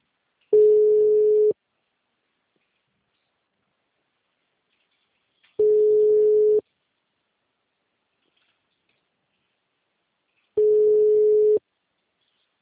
Ringback tone (Europe, 425 Hz).